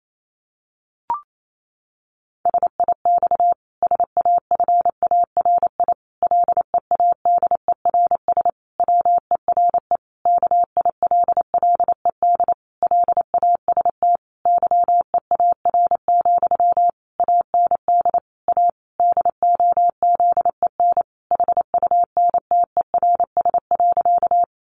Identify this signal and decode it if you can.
Morse code, heard as a keyed tone, about 28 wpm, 700 Hz. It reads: SIX SAFARI LEADERS WERE KILLED LAST YEAR, AND A DOZEN HUNTERS.
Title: A Sound Of Thunder 28wpm